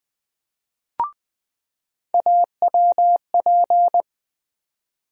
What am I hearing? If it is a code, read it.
Morse code, heard as a keyed tone, about 20 wpm, 700 Hz. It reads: AWP
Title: USA Call Sign Postfixes - 3 Letters 20wpm